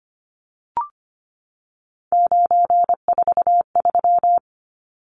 Morse code, heard as a keyed tone, about 25 wpm, 700 Hz. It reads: 943
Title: Numbers - 3-Digit 25wpm